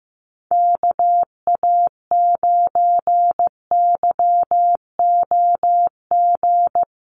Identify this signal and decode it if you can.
Morse code, heard as a keyed tone, about 14 wpm, 700 Hz. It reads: KA9YOG
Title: USA Call Signs Repeated 2x 15wpm